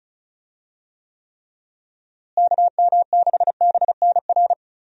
Morse code, heard as a keyed tone, about 34 wpm, 700 Hz. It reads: KM6BNR
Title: USA Call Signs - 6 Characters - Repeated 3x - 35wpm